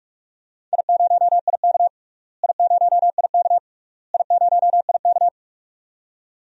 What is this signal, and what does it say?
Morse code, heard as a keyed tone, about 42 wpm, 690 Hz. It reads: I0IK I0IK I0IK
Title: DXCC-100 Call Signs - 4 Characters - Repeated 3x - 45wpm